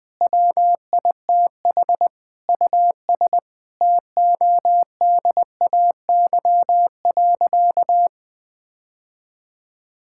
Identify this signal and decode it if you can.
Morse code, heard as a keyed tone, about 20 wpm, 700 Hz. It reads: WITH US TODAY.
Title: Sentences with 9-Word Phrases - Spoken Beginning - 20wpm